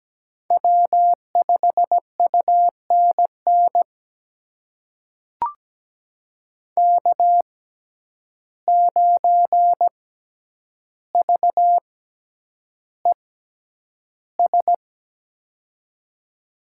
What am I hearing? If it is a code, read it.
Morse code, heard as a keyed tone, about 17 wpm, spaced as for 9 wpm, 700 Hz. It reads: W5UNN K9VES
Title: USA Call Signs - 5 Characters - 6x Character Spacing - 17wpm